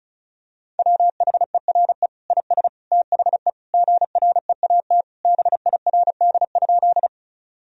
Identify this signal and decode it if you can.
Morse code, heard as a keyed tone, about 36 wpm, 700 Hz. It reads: WHERE IS THE GREAT BIRD?
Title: Sentences from Top 1000 Words - Review Part 2 - 35wpm